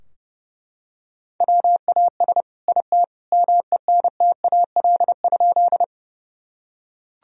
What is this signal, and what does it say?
Morse code, heard as a keyed tone, about 32 wpm, 700 Hz. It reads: WAS IT MENTAL?
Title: Sentences from Top 1000 Words - Review Part 4 - 30wpm